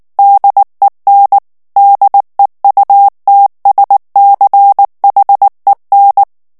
Morse code, heard as a keyed tone, about 19 wpm, 800 Hz. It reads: DEN DEUTSCHEN
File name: morse mp3 / ALLEMAND texte clair 2 vitesse 18 mots minute